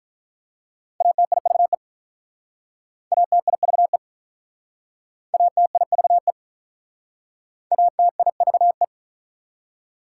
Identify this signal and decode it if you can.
Morse code, heard as a keyed tone, about 46 wpm, 700 Hz. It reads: ATIVE ATIVE ATIVE ATIVE